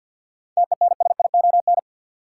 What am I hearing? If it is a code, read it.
Morse code, heard as a keyed tone, about 53 wpm, 700 Hz. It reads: TENSION
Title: Sentences from Top 1600 Words - New Word Drill Alt - 50wpm